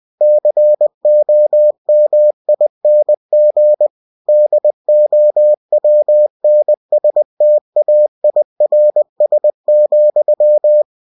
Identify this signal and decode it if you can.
Morse code, heard as a keyed tone, about 20 wpm, 600 Hz. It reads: COMING DOWNSTAIRS,
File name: cw-books / winnie-the-pooh / winnie-20wpm-page-0000